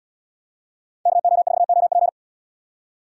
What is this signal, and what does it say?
Morse code, heard as a keyed tone, about 74 wpm, 690 Hz. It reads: KO4OP